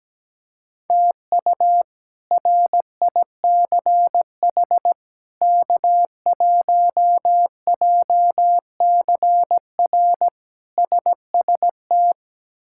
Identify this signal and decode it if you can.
Morse code, heard as a keyed tone, about 17 wpm, 700 Hz. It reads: TU RICH K1JCR SST
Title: K1USN SST QSO Practice - 17wpm